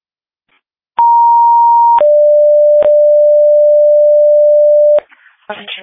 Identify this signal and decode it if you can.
Two-tone page, heard as a one-second tone, then a three-second tone, about 940 then 590 Hz, with speech. It is heard from 1.0 to 5.0 s.